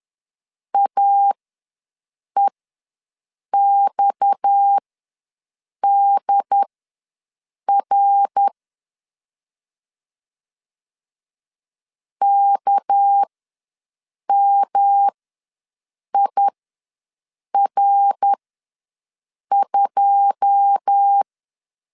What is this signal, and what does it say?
Morse code, heard as a keyed tone, about 10 wpm, spaced as for 6 wpm, 800 Hz. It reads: AEXDR KMIR2
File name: LEZIONI / 9 2 ws 10wpm 4130